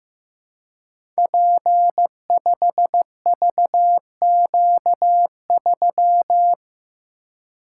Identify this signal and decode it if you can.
Morse code, heard as a keyed tone, about 15 wpm, 700 Hz. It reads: P5VQ3
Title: Code Groups - 5 Letter:Numbers - 15wpm